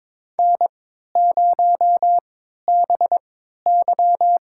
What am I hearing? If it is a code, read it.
Morse code, heard as a keyed tone, about 22 wpm, spaced as for 12 wpm, 700 Hz. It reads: N0BY
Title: USA Call Signs - 4 Characters - 3x Character Spacing - 22wpm